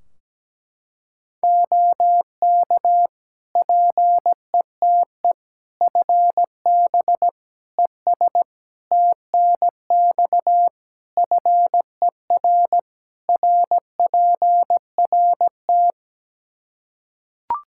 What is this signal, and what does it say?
Morse code, heard as a keyed tone, about 17 wpm, 700 Hz. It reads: OK PETE FB ES TNX FER RPRT
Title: Practice QSO - Exchange 2 - Info - 17wpm